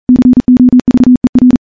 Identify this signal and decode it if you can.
Morse code, heard as a keyed tone, about 40 wpm, 250 Hz. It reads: COVER
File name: morse / morse-training-40WPM